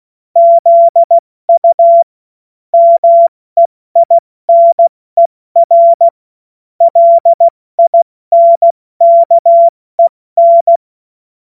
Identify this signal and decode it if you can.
Morse code, heard as a keyed tone, about 16 wpm, 680 Hz. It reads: ZU MEINER LINKEN
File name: Morsekurs / Witze / Witz 09. Februar 2021, 70 F80